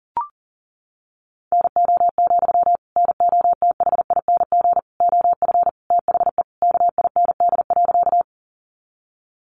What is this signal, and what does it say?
Morse code, heard as a keyed tone, about 40 wpm, 700 Hz. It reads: NO, NOTHING OF THE KIND.
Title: Sentences from Top 500 Words 40wpm